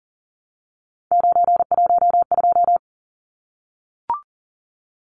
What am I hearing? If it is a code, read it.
Morse code, heard as a keyed tone, about 40 wpm, 700 Hz. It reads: 912
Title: Numbers - 3-Digit 40wpm